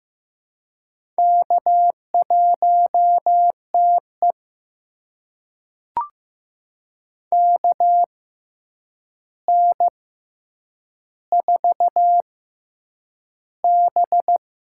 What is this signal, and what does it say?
Morse code, heard as a keyed tone, about 15 wpm, spaced as for 8 wpm, 700 Hz. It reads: K1TE KN4B
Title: USA Call Signs - 4 Characters - 6x Character Spacing - 15wpm